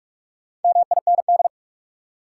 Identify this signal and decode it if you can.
Morse code, heard as a keyed tone, about 46 wpm, 700 Hz. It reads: MIND
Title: Sets of 5 Words - New Word Drill - 45wpm